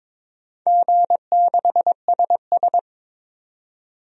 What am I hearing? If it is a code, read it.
Morse code, heard as a keyed tone, about 22 wpm, 700 Hz. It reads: G6SS
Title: DXCC-100 Call Signs - 4 Characters - 22wpm